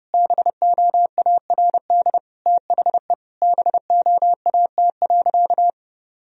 Morse code, heard as a keyed tone, about 32 wpm, 700 Hz. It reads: BOARD THE BOAT.
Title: Sentences from Top 1000 Words - Review Part 4 - 30wpm